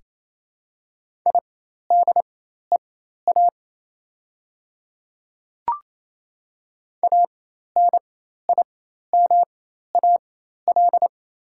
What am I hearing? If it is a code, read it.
Morse code, heard as a keyed tone, about 28 wpm, spaced as for 13 wpm, 700 Hz. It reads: IDEA ANIMAL
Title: 4x Character Spacing - Top 300 Words - 28wpm